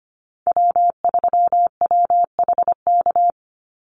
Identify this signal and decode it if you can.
Morse code, heard as a keyed tone, about 25 wpm, 700 Hz. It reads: W3WHK
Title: Call Signs - 25wpm